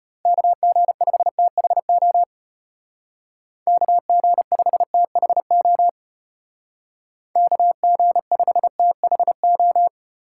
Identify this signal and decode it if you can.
Morse code, heard as a keyed tone, about 34 wpm, 700 Hz. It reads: KG5THO KG5THO KG5THO